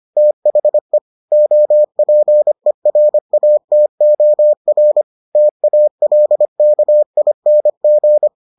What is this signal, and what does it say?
Morse code, heard as a keyed tone, about 25 wpm, 600 Hz. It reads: THE OPERATOR TALKING